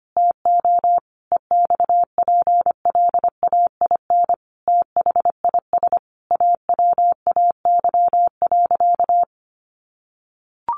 Morse code, heard as a keyed tone, about 25 wpm, 700 Hz. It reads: TO EXPLAIN THIS AWAY.
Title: Sentences from Top 1100 Words 25wpm